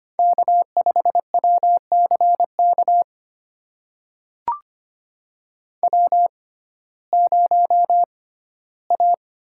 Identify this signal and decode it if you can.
Morse code, heard as a keyed tone, about 25 wpm, spaced as for 15 wpm, 700 Hz. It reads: K5WCK W0A
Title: USA Call Signs - 6x Character Spacing - 25wpm